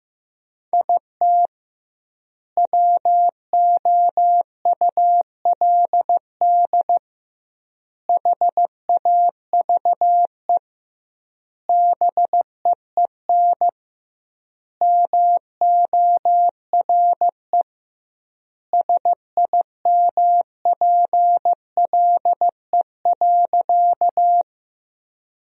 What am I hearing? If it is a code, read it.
Morse code, heard as a keyed tone, about 15 wpm, 700 Hz. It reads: IT WOULD HAVE BEEN MORE SIMPLE.
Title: 2x Word-Spacing - Sentences from Top 400 Words - 15wpm